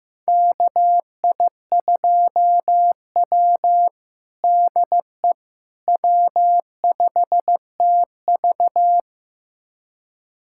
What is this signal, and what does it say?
Morse code, heard as a keyed tone, about 15 wpm, 700 Hz. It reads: KI2W DE W5TV
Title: USA Callsign DE Callsign - 4 Characters - 15wpm